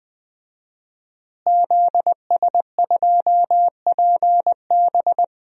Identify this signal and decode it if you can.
Morse code, heard as a keyed tone, about 20 wpm, 700 Hz. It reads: ZS2PB
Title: DXCC-100 Call Signs - 5 Characters - Repeated 2x - 20wpm